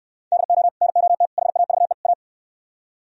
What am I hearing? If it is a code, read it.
Morse code, heard as a keyed tone, about 68 wpm, 700 Hz. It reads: DO NOT HAVE A